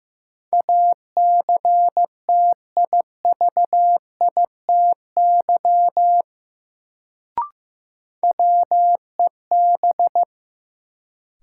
Morse code, heard as a keyed tone, about 15 wpm, 700 Hz. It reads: ACTIVITY WEB
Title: Sentences from Top 1000 Words - Review Part 4 - New Word Drill - 15wpm